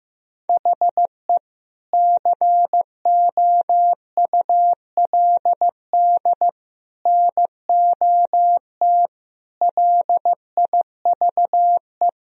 Morse code, heard as a keyed tone, about 15 wpm, 700 Hz. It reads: HE COULD NOT LIVE
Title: Sentences from Top 1000 Words - Review Part 3 - 15wpm